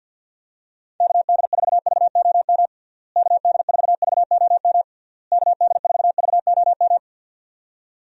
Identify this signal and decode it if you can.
Morse code, heard as a keyed tone, about 48 wpm, 700 Hz. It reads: KD4VOM KD4VOM KD4VOM
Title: USA Call Signs - 6 Characters - Repeated 3x - 50wpm